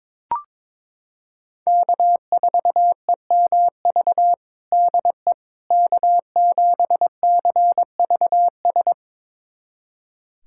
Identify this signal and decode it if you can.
Morse code, heard as a keyed tone, about 22 wpm, 700 Hz. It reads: K4EMV DE K7CVS
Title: USA Callsign DE Callsign - 5 Characters - 22wpm